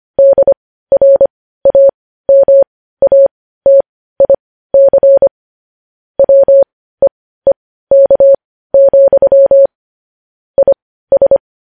Morse code, heard as a keyed tone, about 25 wpm, spaced as for 15 wpm, 550 Hz. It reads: DRAMATIC WEEK, IS
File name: podcast / News headlines-1x-25x15